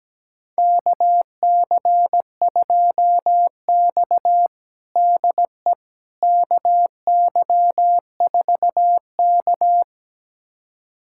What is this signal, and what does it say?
Morse code, heard as a keyed tone, about 17 wpm, 700 Hz. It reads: KC2X DE KY4K
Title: USA Callsign DE Callsign - 4 Characters - 17wpm